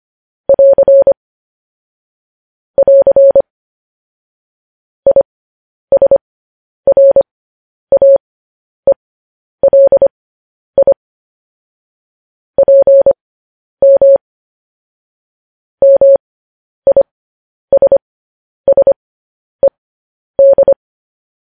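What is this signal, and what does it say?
Morse code, heard as a keyed tone, about 25 wpm, spaced as for 10 wpm, 550 Hz. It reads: <AR> <AR> ISRAELI PM MISSED
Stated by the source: Morse code news headlines practice transmission.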